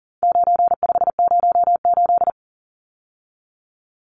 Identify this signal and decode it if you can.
Morse code, heard as a keyed tone, about 40 wpm, 700 Hz. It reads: 9508
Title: Numbers - 4-Digit 40wpm